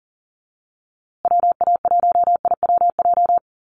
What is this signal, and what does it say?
Morse code, heard as a keyed tone, about 40 wpm, 700 Hz. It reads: WA1IWJ
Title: USA Call Signs Repeated 2x 40wpm